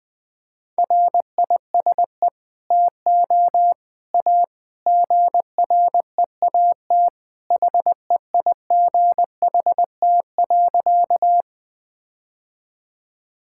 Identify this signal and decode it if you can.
Morse code, heard as a keyed tone, about 20 wpm, 700 Hz. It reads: RISE TO A GREAT HEIGHT.
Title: Sentences from Top 1600 Words 20wpm